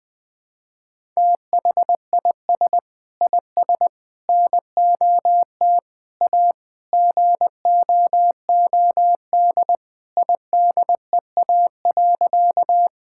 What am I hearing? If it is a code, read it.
Morse code, heard as a keyed tone, about 21 wpm, 700 Hz. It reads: THIS IS NOT A GOOD IDEA.
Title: Sentences with 4-Word Phrases - 20wpm